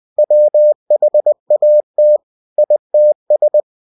Morse code, heard as a keyed tone, about 20 wpm, 600 Hz. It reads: WHAT ITS
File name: cw-books / winnie-the-pooh / winnie-20wpm-page-0102